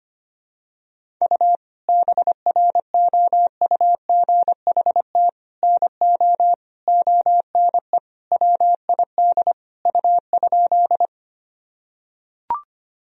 Morse code, heard as a keyed tone, about 25 wpm, 700 Hz. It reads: U BROUGHT NO ONE WID U?
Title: Sentences from Top 500 Words and Abbreviations 25wpm